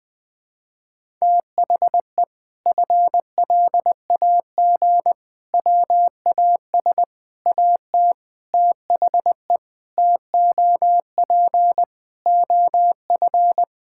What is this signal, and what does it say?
Morse code, heard as a keyed tone, about 20 wpm, 700 Hz. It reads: THE FLAG WAS AT THE TOP OF
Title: Sentences with 5-Word Phrases - 20wpm